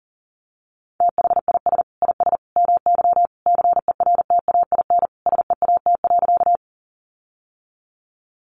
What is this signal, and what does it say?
Morse code, heard as a keyed tone, about 40 wpm, 700 Hz. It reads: THIS IS MY CERTAIN SEAT.
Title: Sentences from Top 1500 Words 40wpm